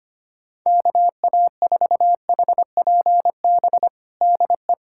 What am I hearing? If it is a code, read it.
Morse code, heard as a keyed tone, about 25 wpm, 700 Hz. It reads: KA4HPB DE
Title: USA Callsign DE Callsign 25wpm